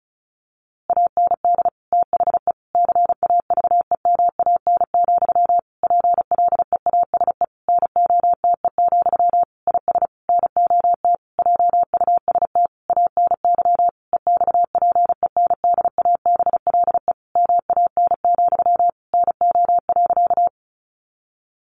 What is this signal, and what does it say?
Morse code, heard as a keyed tone, about 34 wpm, 700 Hz. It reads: AND THE CAVEMAN, PLEASE NOTE, IS NOT JUST ANY EXPENDABLE MAN, NO.
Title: A Sound Of Thunder 35wpm